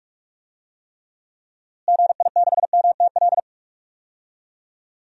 Morse code, heard as a keyed tone, about 46 wpm, 700 Hz. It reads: GI6MTL
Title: DXCC-100 Call Signs - 6 Characters - Repeated 3x - 45wpm